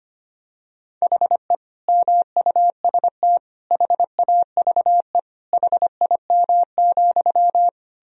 Morse code, heard as a keyed tone, about 25 wpm, 700 Hz. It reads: HE MUST HAVE HIM,
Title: Sentences from Top 1000 Words - Review Part 4 - 25wpm